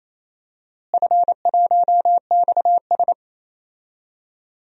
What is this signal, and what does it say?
Morse code, heard as a keyed tone, about 27 wpm, 700 Hz. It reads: F1XS
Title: DXCC-100 Call Signs - 4 Characters - Repeated 2x - 28wpm